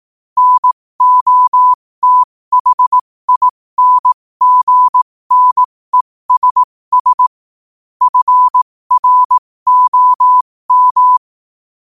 Morse code, heard as a keyed tone, about 18 wpm, spaced as for 15 wpm, 1000 Hz. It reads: NOTHINGNESS FROM
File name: qotdfiles / 15wpm